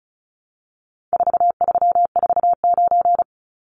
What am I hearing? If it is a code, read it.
Morse code, heard as a keyed tone, about 34 wpm, 700 Hz. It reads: 4349
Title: Numbers - 4-Digit 35wpm